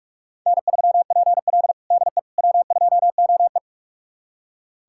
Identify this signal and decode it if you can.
Morse code, heard as a keyed tone, about 43 wpm, 700 Hz. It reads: N3PL DE W2OE
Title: USA Callsign DE Callsign - 4 Characters - 45wpm